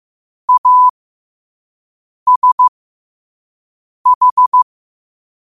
Morse code, heard as a keyed tone, about 15 wpm, spaced as for 5 wpm, 1000 Hz. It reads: ASH
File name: qotdfiles / 5wpm